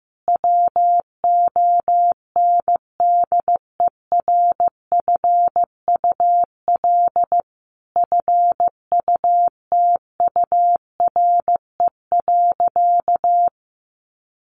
Morse code, heard as a keyed tone, about 15 wpm, 700 Hz. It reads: WONDERFUL FUTURE.
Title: Sentences from Top 1100 Words 15wpm